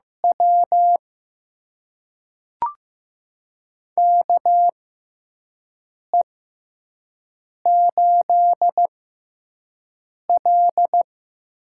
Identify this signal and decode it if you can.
Morse code, heard as a keyed tone, about 15 wpm, spaced as for 5 wpm, 700 Hz. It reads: W KE8L
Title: USA Call Signs - 6x Character Spacing - 15wpm